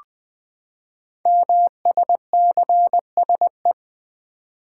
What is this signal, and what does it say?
Morse code, heard as a keyed tone, about 20 wpm, 700 Hz. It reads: MSCSE
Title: Code Groups - 5 Letter:Numbers - 20wpm